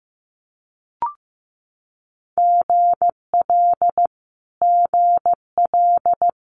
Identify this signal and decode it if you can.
Morse code, heard as a keyed tone, about 14 wpm, 700 Hz. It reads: GL GL
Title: Multi-letter Phonemes Repeated 3x 15wpm